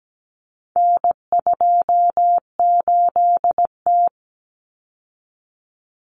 Morse code, heard as a keyed tone, about 17 wpm, 700 Hz. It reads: N28T
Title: Common Cut Numbers - 4-Digit 17wpm